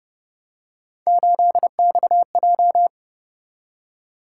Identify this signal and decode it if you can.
Morse code, heard as a keyed tone, about 31 wpm, 700 Hz. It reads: 8XJ